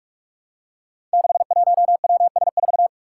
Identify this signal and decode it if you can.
Morse code, heard as a keyed tone, about 46 wpm, 700 Hz. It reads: 61WS4
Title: Five Challenging Letter-Number Combinations 45wpm